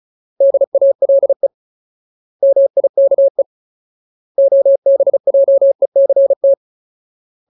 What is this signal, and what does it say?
Morse code, heard as a keyed tone, about 34 wpm, 550 Hz. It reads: DALE MIKE OBJECT
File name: Ultimate Word List / 35WPM Ultimate